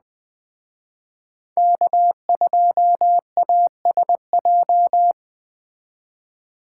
Morse code, heard as a keyed tone, about 20 wpm, 700 Hz. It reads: K2ASJ